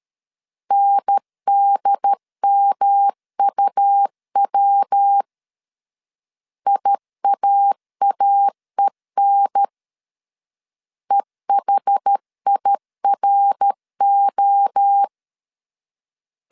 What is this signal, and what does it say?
Morse code, heard as a keyed tone, about 13 wpm, 800 Hz. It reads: NDMUW IAAEN EHIRO
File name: CW / DET / 1 3 1500C S C 12wpm 2750